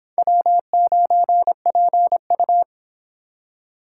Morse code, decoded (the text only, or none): W9PU